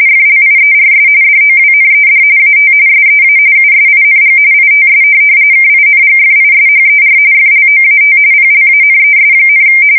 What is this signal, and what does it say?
RTTY, about 45.45 baud. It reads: FLIERS DOWNED AT SEA USING ITS KINGFISHER FLOAT PLANES,
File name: fsk170 / USS-New-Jersey